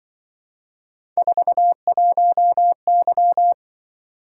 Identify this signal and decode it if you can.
Morse code, heard as a keyed tone, about 24 wpm, 700 Hz. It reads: 41Y
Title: Speed-Racing - Three Letter-Number Combinations - 20wpm